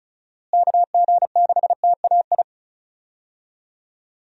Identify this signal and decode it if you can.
Morse code, heard as a keyed tone, about 36 wpm, 700 Hz. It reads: KG6TAI
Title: USA Call Signs - 6 Characters - Repeated 2x - 35wpm